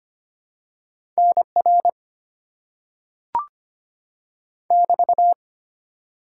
Morse code, heard as a keyed tone, about 25 wpm, 690 Hz. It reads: NR <BT>
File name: Top 90 QSO Elements - 6x Character Spacing - 25wpm